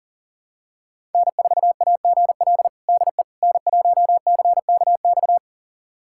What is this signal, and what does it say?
Morse code, heard as a keyed tone, about 40 wpm, 700 Hz. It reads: N4AGL DE N1CKX
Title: USA Callsign DE Callsign - 5 Characters - 40wpm